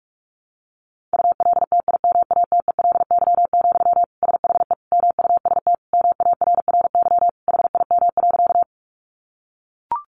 Morse code, heard as a keyed tone, about 44 wpm, 700 Hz. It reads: ULTIMATELY, SHE MUST MARRY HIM.
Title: Sentences from Top 1200 Words 45wpm